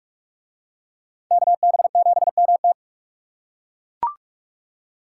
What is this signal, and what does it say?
Morse code, heard as a keyed tone, about 40 wpm, 700 Hz. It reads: KB7MT